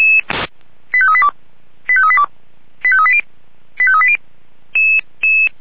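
Five-tone selective call (ZVEI), heard as quick runs of five tones, tones of about 70 ms. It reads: 85382 85382 85380 85380
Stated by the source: ZVEI five-tone selective call received on an SDR.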